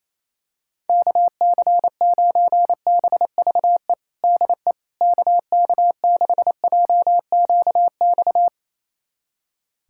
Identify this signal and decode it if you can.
Morse code, heard as a keyed tone, about 28 wpm, 700 Hz. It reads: KC9BVE DE KK6JQX
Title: USA Callsign DE Callsign - 6 Characters - 28wpm